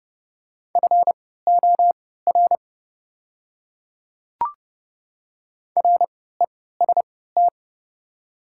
Morse code, heard as a keyed tone, about 30 wpm, spaced as for 17 wpm, 700 Hz. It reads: FOR REST
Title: 3x Character Spacing - Top 500 Words - 30wpm